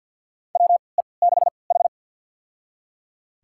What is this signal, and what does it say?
Morse code, heard as a keyed tone, about 52 wpm, spaced as for 29 wpm, 700 Hz. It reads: WE6H
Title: USA Call Signs - 3x Character Spacing - 50wpm